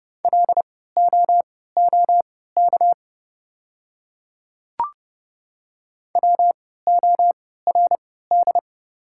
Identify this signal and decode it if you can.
Morse code, heard as a keyed tone, about 30 wpm, spaced as for 17 wpm, 700 Hz. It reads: LOOK WORD
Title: 3x Character Spacing - Top 100 Words - 30wpm